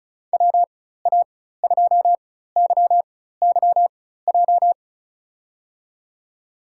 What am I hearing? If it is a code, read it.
Morse code, heard as a keyed tone, about 36 wpm, spaced as for 16 wpm, 700 Hz. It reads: WA2YYJ
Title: USA Call Signs - 6 Characters - 4x Character Spacing - 35wpm